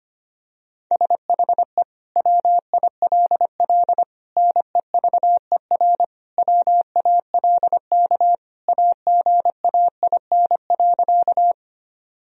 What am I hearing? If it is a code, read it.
Morse code, heard as a keyed tone, about 25 wpm, 700 Hz. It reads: SHE WILL NEVER WALK AGAIN.
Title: Sentences from Top 1000 Words - Review Part 1 - 25wpm